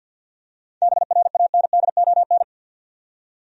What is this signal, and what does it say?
Morse code, heard as a keyed tone, about 50 wpm, 700 Hz. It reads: BRANDON
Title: USA Common Names - 50wpm